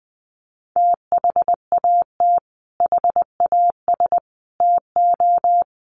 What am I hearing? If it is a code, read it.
Morse code, heard as a keyed tone, about 20 wpm, 700 Hz. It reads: THAT HAS TO
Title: Sets of 5 Words Encore2 20wpm